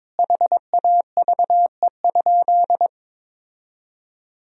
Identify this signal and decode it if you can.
Morse code, heard as a keyed tone, about 22 wpm, 700 Hz. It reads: HAVE?
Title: Sentences from Top 1000 Words - Review Part 3 - 22wpm